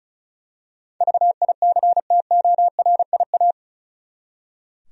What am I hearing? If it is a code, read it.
Morse code, heard as a keyed tone, about 36 wpm, 700 Hz. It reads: VICTORIA
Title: USA Common Names - 35wpm